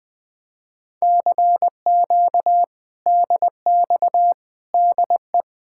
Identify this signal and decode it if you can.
Morse code, heard as a keyed tone, about 20 wpm, 700 Hz. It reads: CQ DX DE
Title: Speed-Racing Common QSO Phrases - 20wpm